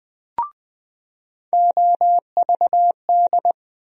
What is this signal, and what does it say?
Morse code, heard as a keyed tone, about 20 wpm, 700 Hz. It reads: OVD
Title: USA Call Sign Postfixes - 3 Letters 20wpm